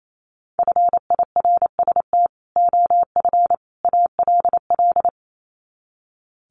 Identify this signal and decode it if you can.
Morse code, heard as a keyed tone, about 28 wpm, 700 Hz. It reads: FIRST OF ALL
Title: Sets of 3 Words 28wpm